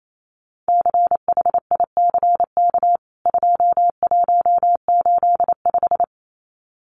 Morse code, heard as a keyed tone, about 28 wpm, 700 Hz. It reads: CHICK 2185
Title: CWT Exchange 28wpm